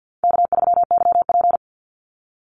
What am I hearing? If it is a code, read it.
Morse code, heard as a keyed tone, about 50 wpm, 700 Hz. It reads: K3YP
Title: Call Signs Repeated 2x - 50wpm